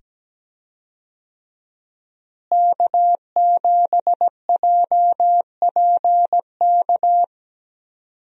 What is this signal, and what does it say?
Morse code, heard as a keyed tone, about 17 wpm, 700 Hz. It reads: K7JPK
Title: USA Call Signs - 5 Characters - Repeated 2x - 17wpm